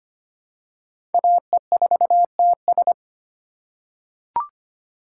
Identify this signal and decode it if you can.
Morse code, heard as a keyed tone, about 25 wpm, 700 Hz. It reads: AE4TS
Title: USA Call Signs - 5 Characters - 4x Character Spacing - 25wpm